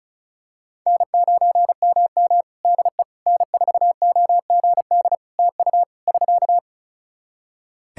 Morse code, heard as a keyed tone, about 34 wpm, 700 Hz. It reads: N9MM DE N4OGD TU <SK>
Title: K1USN SST QSO Practice 35wpm